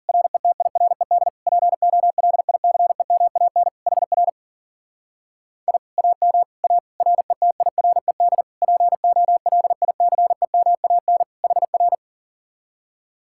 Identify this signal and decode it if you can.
Morse code, heard as a keyed tone, about 50 wpm, 700 Hz. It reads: RETIRED POLICEMAN HR I AM A RETIRED POLICEMAN HR
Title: Speed-Racing Common QSO Phrases - 40wpm